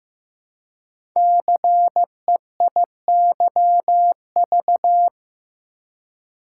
Morse code, heard as a keyed tone, about 15 wpm, 700 Hz. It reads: CEIYV
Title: Code Groups - 5 Letter:Numbers - 15wpm